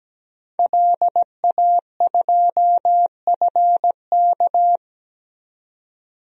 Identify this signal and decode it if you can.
Morse code, heard as a keyed tone, about 17 wpm, 700 Hz. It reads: LA2FK
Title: DXCC-100 Call Signs - 5 Characters - 17wpm